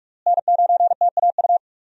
Morse code, heard as a keyed tone, about 46 wpm, 700 Hz. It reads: N9TAU
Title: USA Call Signs - 5 Characters - Repeated 2x - 45wpm